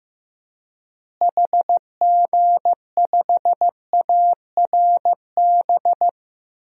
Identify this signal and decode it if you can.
Morse code, heard as a keyed tone, about 15 wpm, 700 Hz. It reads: HG5ARB
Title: DXCC-100 Call Signs - 6 Characters - Repeated 2x - 15wpm